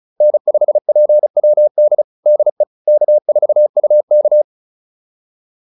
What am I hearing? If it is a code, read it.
Morse code, heard as a keyed tone, about 34 wpm, 600 Hz. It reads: N5PWD DE K4UK
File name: QSO208 35